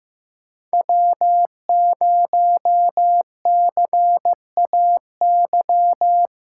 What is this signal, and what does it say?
Morse code, heard as a keyed tone, about 15 wpm, 700 Hz. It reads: W0CAY
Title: USA Callsign DE Callsign - 5 Characters - 15wpm